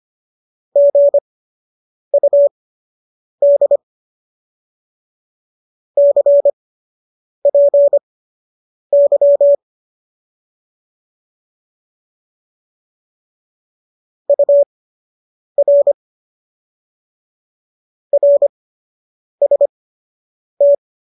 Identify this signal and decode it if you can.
Morse code, heard as a keyed tone, about 25 wpm, spaced as for 8 wpm, 570 Hz. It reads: GUD CPY UR RST
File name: qso003 08